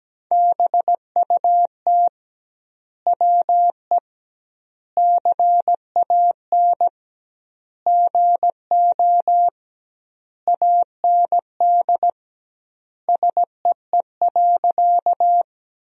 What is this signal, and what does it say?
Morse code, heard as a keyed tone, about 17 wpm, 700 Hz. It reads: BUT WE CAN GO AND SEE.
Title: 2x Word-Spacing - Sentences from Top 100 Words - 17wpm